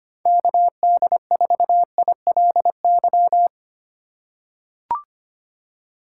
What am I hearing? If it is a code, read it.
Morse code, heard as a keyed tone, about 25 wpm, 700 Hz. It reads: KD4ILY
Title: USA Call Signs - 6 Characters - 2x Character Spacing - 25wpm